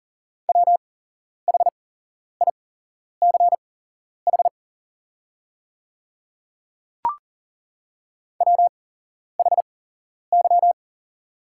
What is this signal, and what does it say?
Morse code, heard as a keyed tone, about 40 wpm, spaced as for 11 wpm, 690 Hz. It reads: WHICH WHY
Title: 8x Character Spacing - Top 500 Words - 40wpm